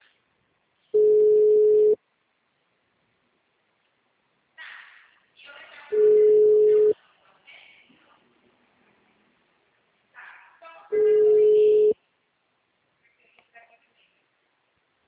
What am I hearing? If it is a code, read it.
Ringback tone (Europe, 425 Hz), with speech.